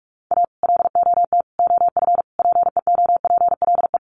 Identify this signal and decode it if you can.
Morse code, heard as a keyed tone, about 44 wpm, 700 Hz. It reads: A LOT OF PEOPLE
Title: Sets of 4 Words 45wpm